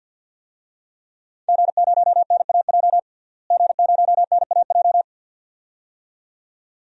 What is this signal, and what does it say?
Morse code, heard as a keyed tone, about 53 wpm, 700 Hz. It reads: G0NAJ G0NAJ
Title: DXCC-100 Call Signs - 5 Characters - Repeated 2x - 50wpm